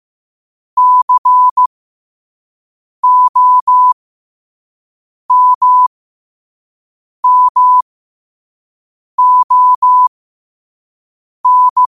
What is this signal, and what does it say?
Morse code, heard as a keyed tone, about 15 wpm, spaced as for 5 wpm, 1000 Hz. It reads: COMMON
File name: qotdfiles / 5wpm